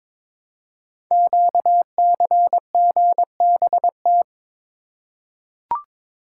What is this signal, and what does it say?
Morse code, heard as a keyed tone, about 22 wpm, 700 Hz. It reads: QCGBT
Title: Code Groups - 5 Letter:Numbers - 22wpm